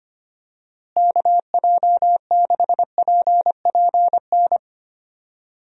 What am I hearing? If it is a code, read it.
Morse code, heard as a keyed tone, about 25 wpm, 700 Hz. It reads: KJ6PPN
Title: USA Call Signs - 6 Characters - Repeated 2x - 25wpm